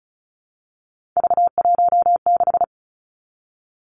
Morse code, heard as a keyed tone, about 34 wpm, 700 Hz. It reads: V16
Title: Three Challenging Letter-Number Combinations 35wpm